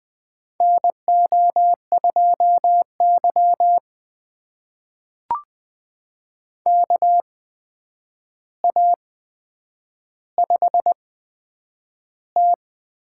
Morse code, heard as a keyed tone, about 20 wpm, spaced as for 9 wpm, 700 Hz. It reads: NO2Y KA5T
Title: USA Call Signs - 4 Characters - 8x Character Spacing - 20wpm